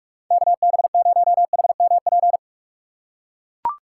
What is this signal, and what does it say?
Morse code, heard as a keyed tone, about 43 wpm, 700 Hz. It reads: KB0HMP
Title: USA Call Signs - 6 Characters - 2x Character Spacing - 45wpm